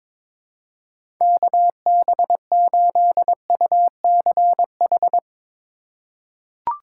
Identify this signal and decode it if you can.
Morse code, heard as a keyed tone, about 22 wpm, 700 Hz. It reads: KB8UCH
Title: USA Call Signs - 6 Characters - 3x Character Spacing - 22wpm